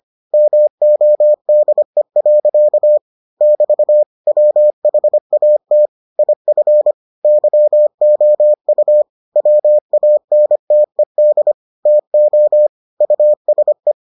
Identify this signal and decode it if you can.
Morse code, heard as a keyed tone, about 25 wpm, 600 Hz. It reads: MODE. <BT> WHAT IF YOU WANTED TO USE